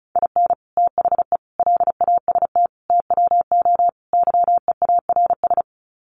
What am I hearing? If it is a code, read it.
Morse code, heard as a keyed tone, about 34 wpm, 700 Hz. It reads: IN THE LAST TWO YEARS
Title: Sets of 5 Words 35wpm